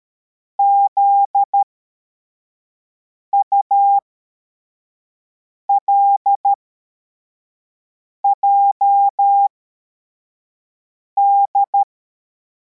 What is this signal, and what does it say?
Morse code, heard as a keyed tone, about 13 wpm, spaced as for 4 wpm, 800 Hz. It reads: ZULJD
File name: formation / cw / Text84